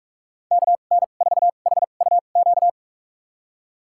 Morse code, heard as a keyed tone, about 46 wpm, spaced as for 33 wpm, 700 Hz. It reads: KN4HUQ
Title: USA Call Signs - 6 Characters - 2x Character Spacing - 45wpm